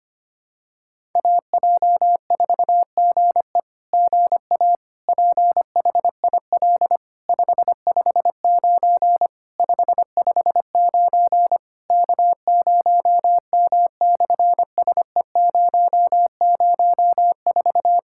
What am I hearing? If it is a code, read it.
Morse code, heard as a keyed tone, about 24 wpm, 700 Hz. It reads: AJ4GE GA PHIL 559 559 K0M/SE004
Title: SOTA QSO Practice 25wpm